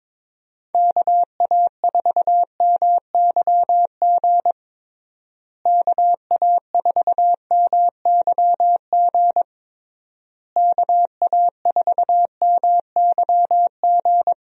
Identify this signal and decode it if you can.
Morse code, heard as a keyed tone, about 22 wpm, 700 Hz. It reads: KA4MYG KA4MYG KA4MYG
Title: USA Call Signs - 6 Characters - Repeated 3x - 22wpm